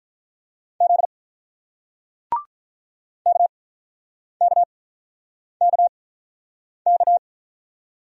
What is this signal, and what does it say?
Morse code, heard as a keyed tone, about 48 wpm, spaced as for 9 wpm, 700 Hz. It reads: Z KKKK